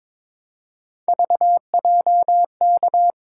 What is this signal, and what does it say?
Morse code, heard as a keyed tone, about 26 wpm, 700 Hz. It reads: VJK
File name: Speed-Racing - Three Letter-Number Combinations - 22wpm